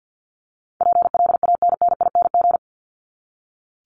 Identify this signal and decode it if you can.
Morse code, heard as a keyed tone, about 50 wpm, 700 Hz. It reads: PLANNING
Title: ING Suffix - Top 100 Words with Base Words 50wpm